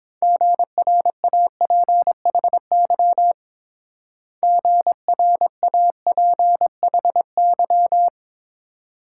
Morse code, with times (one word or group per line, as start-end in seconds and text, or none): 0.2-3.3 GRAPHY
4.4-8.1 GRAPHY